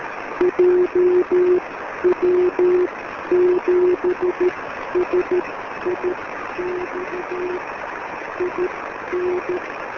Morse code, heard as a keyed tone, about 13 wpm, spaced as for 10 wpm, 360 Hz, with noise. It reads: JW7SIX IN
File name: JW7SIX B